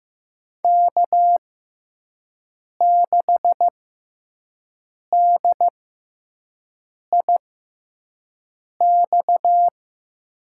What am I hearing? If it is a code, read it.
Morse code, heard as a keyed tone, about 15 wpm, spaced as for 5 wpm, 700 Hz. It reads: K6DIX
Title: USA Call Signs - 5 Characters - 6x Character Spacing - 15wpm